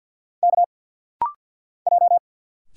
Morse code, heard as a keyed tone, about 50 wpm, 700 Hz. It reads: K J
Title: Single Letters - Rapid-Fire 50wpm